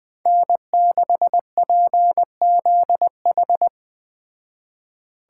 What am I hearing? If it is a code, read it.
Morse code, heard as a keyed tone, about 21 wpm, 700 Hz. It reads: N6PZH